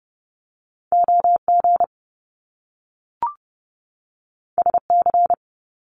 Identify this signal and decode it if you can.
Morse code, heard as a keyed tone, about 30 wpm, 700 Hz. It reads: OG SC
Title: Top Two Letter Combinations in Words 30wpm